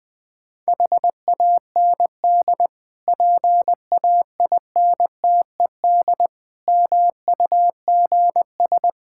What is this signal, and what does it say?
Morse code, heard as a keyed tone, about 20 wpm, 700 Hz. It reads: HAND PAINTED MUGS
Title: Difficult Words with Hints - 20wpm